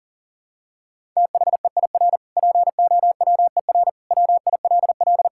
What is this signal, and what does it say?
Morse code, heard as a keyed tone, about 37 wpm, 700 Hz. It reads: THEIR POWER WILL
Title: Sentences from Top 1000 Words - Review Part 6 - 40wpm